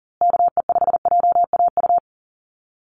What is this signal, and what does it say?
Morse code, heard as a keyed tone, about 40 wpm, 700 Hz. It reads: KE5JAU
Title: USA Call Signs Repeated 2x 40wpm